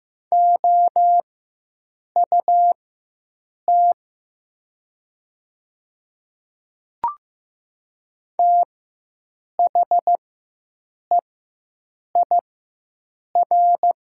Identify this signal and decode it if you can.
Morse code, heard as a keyed tone, about 15 wpm, spaced as for 7 wpm, 700 Hz. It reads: OUT THEIR
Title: 4x Character Spacing - Top 100 Words - 15wpm